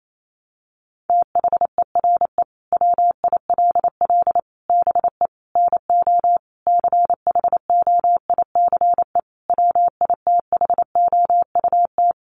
Morse code, heard as a keyed tone, about 28 wpm, 700 Hz. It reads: THERE WILL BE NO CHOICE WITHOUT
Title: Sentences from Top 1500 Words 28wpm